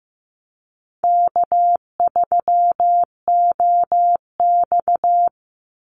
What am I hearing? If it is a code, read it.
Morse code, heard as a keyed tone, about 15 wpm, 700 Hz. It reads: K3OX
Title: Easy Call Signs - 15wpm